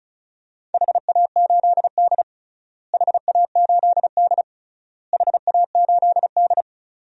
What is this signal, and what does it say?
Morse code, heard as a keyed tone, about 36 wpm, 700 Hz. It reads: HA8D HA8D HA8D
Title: DXCC-100 Call Signs - 4 Characters - Repeated 3x - 35wpm